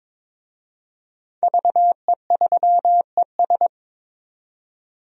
Morse code, heard as a keyed tone, about 22 wpm, 700 Hz. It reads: VE3ES
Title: DXCC-100 Call Signs - 5 Characters - 22wpm